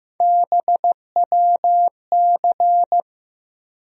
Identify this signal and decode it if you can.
Morse code, heard as a keyed tone, about 15 wpm, 700 Hz. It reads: BWC
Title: USA Call Sign Postfixes - 3 Letters 15wpm